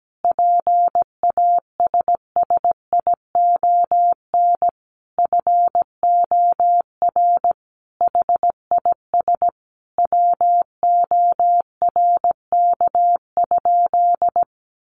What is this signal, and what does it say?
Morse code, heard as a keyed tone, about 17 wpm, 700 Hz. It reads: PASSION FOR HIS WORK?
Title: Sentences from Top 1300 Words 17wpm